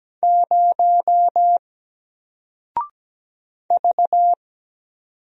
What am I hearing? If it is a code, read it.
Morse code, heard as a keyed tone, about 17 wpm, 700 Hz. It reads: 0 V
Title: Single Character (Letter:Number:Punctuation:Prosign) Alt 17wpm